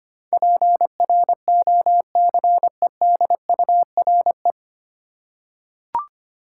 Morse code, heard as a keyed tone, about 25 wpm, 700 Hz. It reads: PROCEDURE
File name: practice / 25wpm / 9 Letter Words - First 3 Letters Given - 25wpm